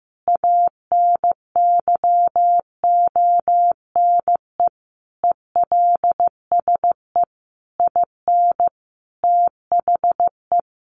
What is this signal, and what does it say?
Morse code, heard as a keyed tone, about 15 wpm, 700 Hz. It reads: ANYONE ELSE IN THE
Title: Sentences from Top 900 Words 15wpm